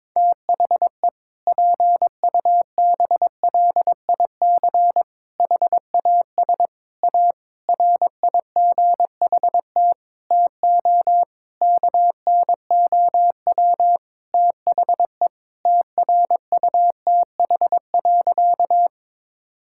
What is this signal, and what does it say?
Morse code, heard as a keyed tone, about 21 wpm, 700 Hz. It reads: THE PUBLIC HAS A RIGHT TO KNOW THE TRUTH.
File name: Sentences with 7-Word Phrases - 22wpm